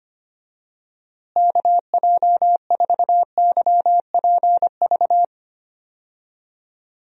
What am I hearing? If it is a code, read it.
Morse code, heard as a keyed tone, about 25 wpm, 700 Hz. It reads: KJ4YPV
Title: USA Call Signs - 6 Characters - Repeated 3x - 25wpm